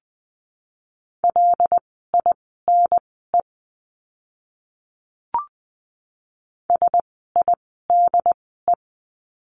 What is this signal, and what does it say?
Morse code, heard as a keyed tone, about 20 wpm, spaced as for 14 wpm, 700 Hz. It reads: LINE SIDE
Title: 115 Lesson 19f - Letter w - Words 20wpm